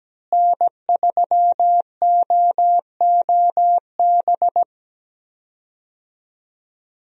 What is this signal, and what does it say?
Morse code, heard as a keyed tone, about 17 wpm, 700 Hz. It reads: N3OOB
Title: USA Call Signs - 5 Characters - Repeated 3x - 17wpm